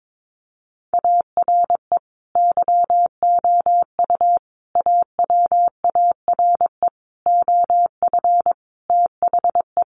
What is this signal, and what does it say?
Morse code, heard as a keyed tone, about 22 wpm, 700 Hz. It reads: ARE YOU AWARE OF THE
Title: Sentences from Top 1500 Words 22wpm